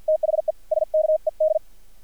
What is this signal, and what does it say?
Morse code, heard as a keyed tone, about 59 wpm, spaced as for 48 wpm, 620 Hz. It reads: THE SKED
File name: archived / CRSnet